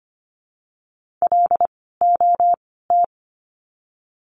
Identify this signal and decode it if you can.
Morse code, heard as a keyed tone, about 25 wpm, 700 Hz. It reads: L O T
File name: course / 25:10wpm / 094 Lesson 16f - Letter c - Words 25wpm